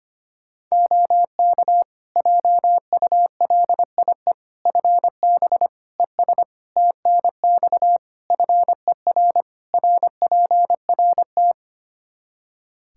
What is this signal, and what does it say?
Morse code, heard as a keyed tone, about 25 wpm, 700 Hz. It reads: OK JULIE FB ES TNX FER RPRT
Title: Practice QSO - Exchange 2 - Info - 25wpm